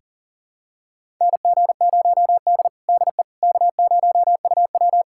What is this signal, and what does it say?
Morse code, heard as a keyed tone, about 40 wpm, 700 Hz. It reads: NG0D DE K0UW
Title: USA Callsign DE Callsign - 4 Characters - 40wpm